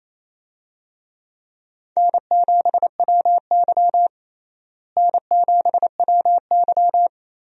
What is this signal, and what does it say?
Morse code, heard as a keyed tone, about 28 wpm, 700 Hz. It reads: N7WY N7WY
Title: USA Call Signs - 4 Characters - Repeated 3x - 28wpm